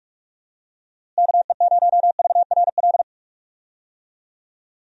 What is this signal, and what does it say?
Morse code, heard as a keyed tone, about 46 wpm, 700 Hz. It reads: KE0VRL
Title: USA Call Signs - 6 Characters - Repeated 3x - 45wpm